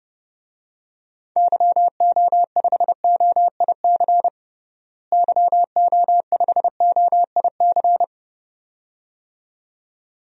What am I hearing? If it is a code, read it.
Morse code, heard as a keyed tone, about 30 wpm, 700 Hz. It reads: YO5OIC YO5OIC
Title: DXCC-100 Call Signs - 6 Characters - Repeated 2x - 30wpm